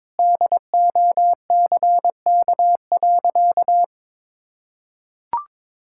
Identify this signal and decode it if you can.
Morse code, heard as a keyed tone, about 22 wpm, 700 Hz. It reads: DOCK.
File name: practice / 22wpm / Sentences with 2-Word Phrases - 22wpm